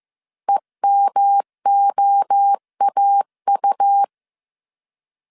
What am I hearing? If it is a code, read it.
Morse code, heard as a keyed tone, about 15 wpm, 800 Hz. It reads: EMOAU
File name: CW / DET / 1 3 1500C S C 14wpm 2458